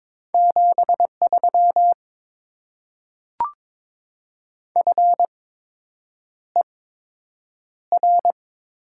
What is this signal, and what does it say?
Morse code, heard as a keyed tone, about 21 wpm, spaced as for 8 wpm, 700 Hz. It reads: 73 FER
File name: Top 20 QSO Elements - 8x Character Spacing - 22wpm